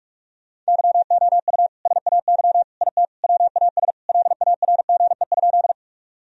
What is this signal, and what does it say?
Morse code, heard as a keyed tone, about 46 wpm, 700 Hz. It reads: YOU SAY IT WAS LARGE?
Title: Sentences from Top 1000 Words - Review Part 1 - 45wpm